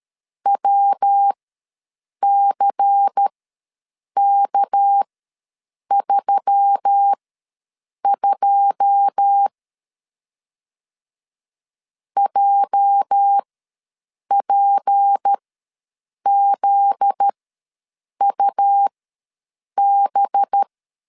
Morse code, heard as a keyed tone, about 13 wpm, spaced as for 7 wpm, 800 Hz. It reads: WCK32 JPZUB
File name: LEZIONI / 9 2 ws 12wpm 3406